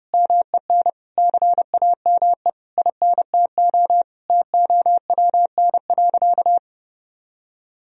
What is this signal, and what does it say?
Morse code, heard as a keyed tone, about 32 wpm, 700 Hz. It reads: MEN CAME INTO TOWN.
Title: Sentences from Top 1000 Words - Review Part 2 - 30wpm